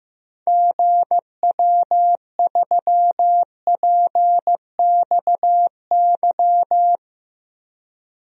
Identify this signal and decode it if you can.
Morse code, heard as a keyed tone, about 15 wpm, 700 Hz. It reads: GW3PXY
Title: DXCC-100 Call Signs - 6 Characters - Repeated 3x - 15wpm